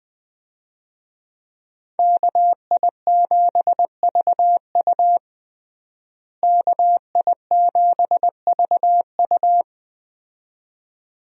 Morse code, heard as a keyed tone, about 20 wpm, 700 Hz. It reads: KI7VU KI7VU
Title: USA Call Signs - 5 Characters - Repeated 2x - 20wpm